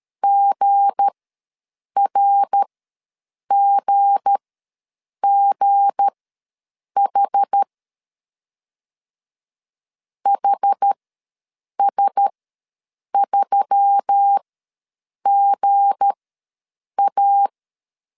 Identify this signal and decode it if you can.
Morse code, heard as a keyed tone, about 13 wpm, spaced as for 7 wpm, 800 Hz. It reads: GRGGH HS3GA
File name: LEZIONI / 7 GF9Z / 7 G ws 12wpm 3140